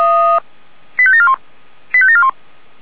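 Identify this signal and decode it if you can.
Five-tone selective call (ZVEI), heard as quick runs of five tones, tones of about 70 ms. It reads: 85741 85741
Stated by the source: ZVEI five-tone call 85721 received by SDR.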